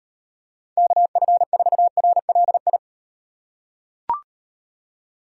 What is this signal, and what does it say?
Morse code, heard as a keyed tone, about 35 wpm, 700 Hz. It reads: KF4RLI